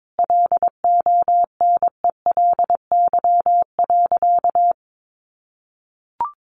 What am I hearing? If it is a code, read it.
Morse code, heard as a keyed tone, about 22 wpm, 700 Hz. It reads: LONELY.
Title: Additional Easy Phrases 22wpm